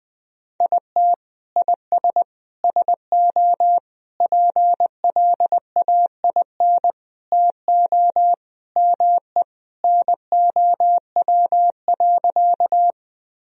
Morse code, heard as a keyed tone, about 20 wpm, 700 Hz. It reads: IT IS SO PLAIN TO ME NOW.
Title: Sentences from Top 1000 Words - Review Part 2 - 20wpm